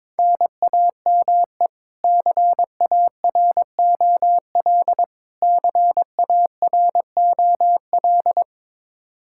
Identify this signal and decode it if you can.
Morse code, heard as a keyed tone, about 22 wpm, 700 Hz. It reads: NAME CAROL CAROL
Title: Practice QSO - Exchange 1 - Beginning - 22wpm